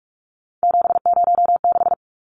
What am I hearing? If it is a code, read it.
Morse code, heard as a keyed tone, about 44 wpm, 700 Hz. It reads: 706
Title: Numbers - 3-Digit 45wpm